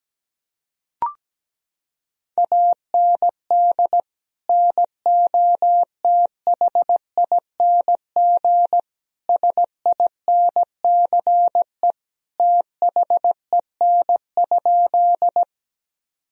Morse code, heard as a keyed tone, about 17 wpm, 700 Hz. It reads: AND NOTHING SINCE THEN?
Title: Sentences from Top 1000 Words - Review Part 1 - 17wpm